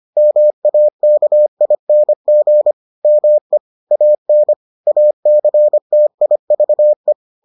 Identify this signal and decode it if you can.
Morse code, heard as a keyed tone, about 25 wpm, 600 Hz. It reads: MAKING ME AN ACTIVE